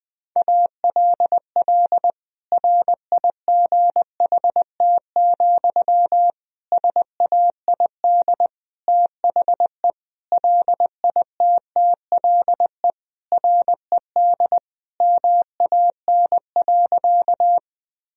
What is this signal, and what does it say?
Morse code, heard as a keyed tone, about 21 wpm, 700 Hz. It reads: ALL RIGHT, SAID THE LITTLE RED MAN.
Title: Sentences from Top 1000 Words - Review Part 2 - 20wpm